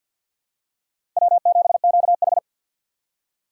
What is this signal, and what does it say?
Morse code, heard as a keyed tone, about 52 wpm, 700 Hz. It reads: W7QH
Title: USA Call Signs - 4 Characters - 3x Character Spacing - 50wpm